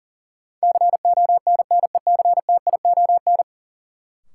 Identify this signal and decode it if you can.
Morse code, heard as a keyed tone, about 40 wpm, 700 Hz. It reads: CONNECTION
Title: Sentences from Top 1300 Words - New Word Drill - 40wpm